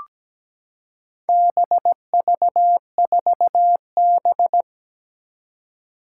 Morse code, heard as a keyed tone, about 17 wpm, 700 Hz. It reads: BV4B
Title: Four Challenging Letter-Number Combinations 17wpm